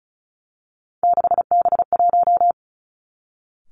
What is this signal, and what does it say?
Morse code, heard as a keyed tone, about 36 wpm, 700 Hz. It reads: 6B1
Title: Three Challenging Letter-Number Combinations 35wpm